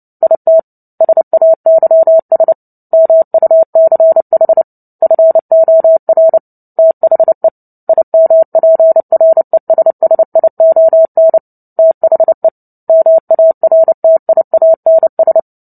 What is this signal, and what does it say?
Morse code, heard as a keyed tone, about 30 wpm, 650 Hz. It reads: IT SAYS MUCH FOR THE IMPRESSION THE MARTIANS
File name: Book Two / 03-Chapter 3 - 29 WPM